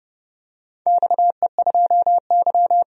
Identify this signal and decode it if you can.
Morse code, heard as a keyed tone, about 30 wpm, 700 Hz. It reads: XE2Y
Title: DXCC-100 Call Signs - 4 Characters - 30wpm